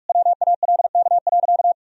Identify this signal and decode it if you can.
Morse code, heard as a keyed tone, about 46 wpm, 700 Hz. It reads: WALK.
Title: Sentences from Top 1000 Words - Review Part 2 - 45wpm